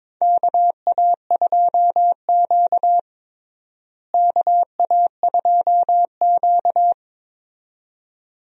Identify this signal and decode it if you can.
Morse code, heard as a keyed tone, about 21 wpm, 700 Hz. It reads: KA2Q KA2Q
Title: USA Call Signs - 4 Characters - Repeated 2x - 22wpm